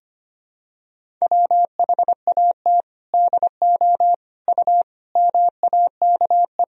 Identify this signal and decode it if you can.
Morse code, heard as a keyed tone, about 25 wpm, 700 Hz. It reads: WHAT DO U MAKE
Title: Sentences from Top 100 Words and Abbreviations 25wpm